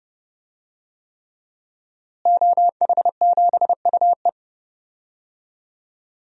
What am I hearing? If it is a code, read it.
Morse code, heard as a keyed tone, about 30 wpm, 700 Hz. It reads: OH7UE
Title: DXCC-100 Call Signs - 5 Characters - Repeated 3x - 30wpm